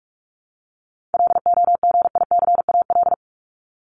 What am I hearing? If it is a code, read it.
Morse code, heard as a keyed tone, about 44 wpm, 700 Hz. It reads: LOGICAL
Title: Top 1501-1750 Words 45wpm